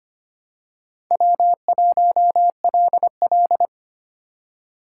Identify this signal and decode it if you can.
Morse code, heard as a keyed tone, about 25 wpm, 700 Hz. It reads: W1LL
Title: USA Call Signs - 4 Characters - 2x Character Spacing - 25wpm